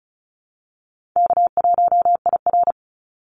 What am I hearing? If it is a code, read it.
Morse code, heard as a keyed tone, about 34 wpm, 700 Hz. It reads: K1IR
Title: Call Signs Repeated 3x - 35wpm